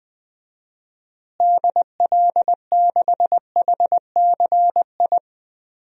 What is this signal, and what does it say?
Morse code, heard as a keyed tone, about 21 wpm, 700 Hz. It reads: DL6HCI
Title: DXCC-100 Call Signs - 6 Characters - Repeated 3x - 20wpm